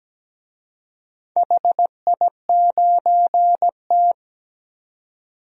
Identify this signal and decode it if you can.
Morse code, heard as a keyed tone, about 17 wpm, 700 Hz. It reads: HI9T
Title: DXCC-100 Call Signs - 4 Characters - 17wpm